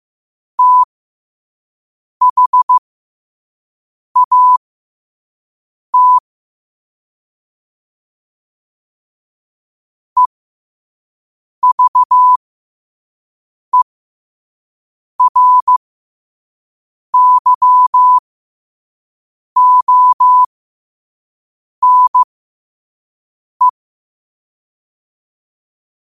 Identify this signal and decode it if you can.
Morse code, heard as a keyed tone, about 15 wpm, spaced as for 5 wpm, 1000 Hz. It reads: THAT EVERYONE
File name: qotdfiles / 5wpm